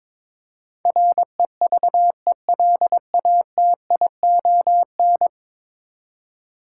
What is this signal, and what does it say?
Morse code, heard as a keyed tone, about 22 wpm, 700 Hz. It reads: REVELATION
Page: morse-code-ninja-practice-22wpm